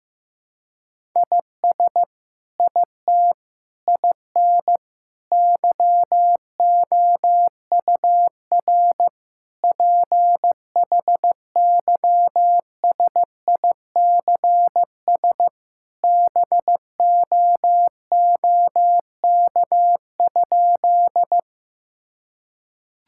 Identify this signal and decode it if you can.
Morse code, heard as a keyed tone, about 16 wpm, 700 Hz. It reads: IS IT IN YOUR PHYSICS BOOK?
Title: Sentences from Top 1000 Words - Review Part 5 - 15wpm